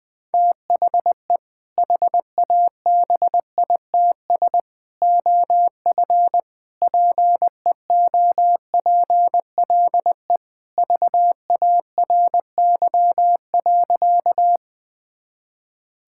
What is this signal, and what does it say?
Morse code, heard as a keyed tone, about 20 wpm, 700 Hz. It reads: THE HABITS OF PEOPLE VARY.
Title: Sentences with 2-Word Phrases - 20wpm